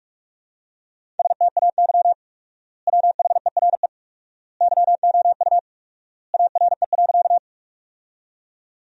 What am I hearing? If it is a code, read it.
Morse code, heard as a keyed tone, about 46 wpm, 700 Hz. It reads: STAY WHERE YOU ARE.
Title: 4x Word-Spacing - Sentences from Top 500 Words - 45wpm